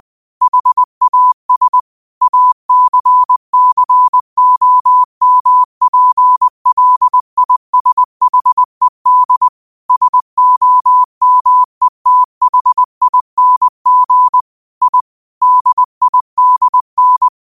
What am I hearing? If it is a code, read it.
Morse code, heard as a keyed tone, about 20 wpm, 1000 Hz. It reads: HAS ACCOMPLISHED SOMETHING I DIDN
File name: qotdfiles / 20wpm